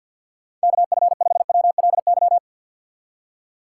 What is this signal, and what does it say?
Morse code, heard as a keyed tone, about 52 wpm, 700 Hz. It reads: KF5WLY
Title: USA Call Signs - 6 Characters - 4x Character Spacing - 50wpm